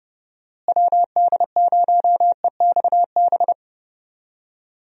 Morse code, heard as a keyed tone, about 30 wpm, 700 Hz. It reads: WD0EXB
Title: USA Call Signs - 6 Characters - Repeated 2x - 30wpm